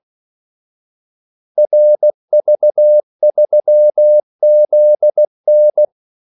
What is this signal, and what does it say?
Morse code, heard as a keyed tone, about 16 wpm, 600 Hz. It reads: RV3ZN
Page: Kaldesignaler 16 wpm